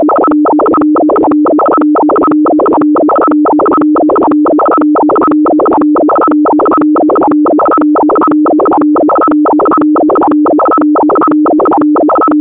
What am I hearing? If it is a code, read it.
Two-tone teleprinter signal, about 200 Hz shift.